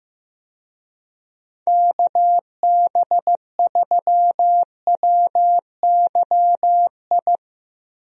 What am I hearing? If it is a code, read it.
Morse code, heard as a keyed tone, about 15 wpm, 700 Hz. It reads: KB3WYI
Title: USA Call Signs - 6 Characters - Repeated 3x - 15wpm